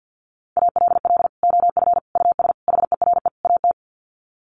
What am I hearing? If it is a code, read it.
Morse code, heard as a keyed tone, about 50 wpm, 700 Hz. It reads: ALL OF US HERE AT
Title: Sets of 5 Words 50wpm